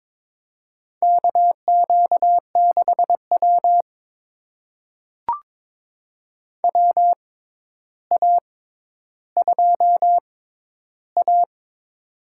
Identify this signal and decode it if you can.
Morse code, heard as a keyed tone, about 22 wpm, spaced as for 11 wpm, 700 Hz. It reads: KQ6W WA2A
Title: USA Call Signs - 4 Characters - 6x Character Spacing - 22wpm